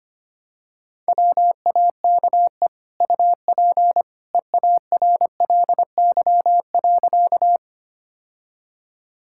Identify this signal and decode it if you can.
Morse code, heard as a keyed tone, about 25 wpm, 700 Hz. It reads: WAKE UP EARLY.
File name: Easy Sentences with Hints - 25wpm